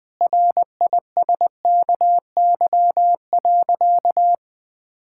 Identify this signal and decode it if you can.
Morse code, heard as a keyed tone, about 20 wpm, 700 Hz. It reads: RISKY.
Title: Sentences with 6-Word Phrases - Spoken Beginning - 20wpm